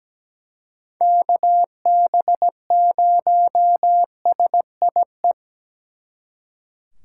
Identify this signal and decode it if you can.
Morse code, heard as a keyed tone, about 20 wpm, 700 Hz. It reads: KB0SIE
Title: Speed Racing - USA Call Signs 17wpm